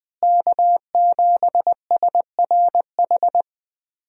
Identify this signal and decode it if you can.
Morse code, heard as a keyed tone, about 20 wpm, 700 Hz. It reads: K7SRH
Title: USA Call Signs - 5 Characters - Repeated 3x - 20wpm